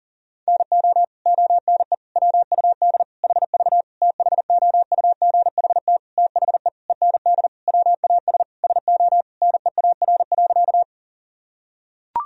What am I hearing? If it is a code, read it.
Morse code, heard as a keyed tone, about 40 wpm, 700 Hz. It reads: NO ONE WUD HV THOUGHT THE END WAS SO NEAR.
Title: Sentences from Top 300 Words and Abbreviations 40wpm